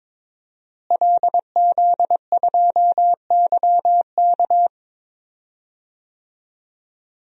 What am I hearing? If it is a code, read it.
Morse code, heard as a keyed tone, about 22 wpm, 700 Hz. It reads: LZ2YK
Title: DXCC-100 Call Signs - 5 Characters - Repeated 2x - 22wpm